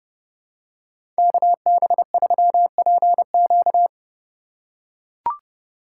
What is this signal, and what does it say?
Morse code, heard as a keyed tone, about 31 wpm, 700 Hz. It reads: KB3PQ
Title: USA Call Signs - 5 Characters - 4x Character Spacing - 30wpm